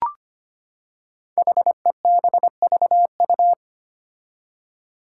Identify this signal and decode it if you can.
Morse code, heard as a keyed tone, about 25 wpm, 700 Hz. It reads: HEBVU
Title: Five Challenging Letter-Number Combinations 25wpm